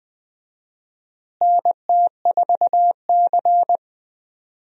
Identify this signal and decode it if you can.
Morse code, heard as a keyed tone, about 21 wpm, 700 Hz. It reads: NT4C
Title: USA Call Signs - 4 Characters - Repeated 2x - 20wpm